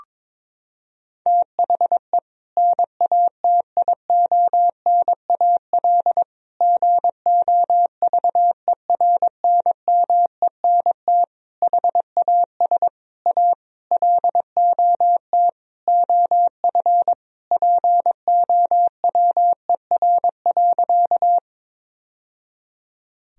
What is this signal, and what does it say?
Morse code, heard as a keyed tone, about 22 wpm, 700 Hz. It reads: THE NATIONAL GOVERNMENT HAS A LOT OF POWER.
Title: Sentences from Top 1000 Words - Review Part 4 - 22wpm